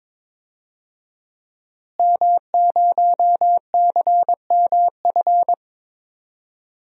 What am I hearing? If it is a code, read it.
Morse code, heard as a keyed tone, about 22 wpm, 700 Hz. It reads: M0CMF
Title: DXCC-100 Call Signs - 5 Characters - Repeated 2x - 22wpm